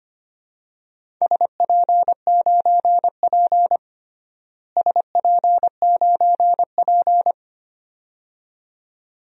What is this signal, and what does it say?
Morse code, heard as a keyed tone, about 25 wpm, 700 Hz. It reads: SP9P SP9P
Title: DXCC-100 Call Signs - 4 Characters - Repeated 2x - 25wpm